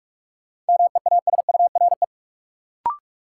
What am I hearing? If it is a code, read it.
Morse code, heard as a keyed tone, about 43 wpm, 700 Hz. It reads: MEASURE
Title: Sentences from Top 1200 Words - New Word Drill Alt - 45wpm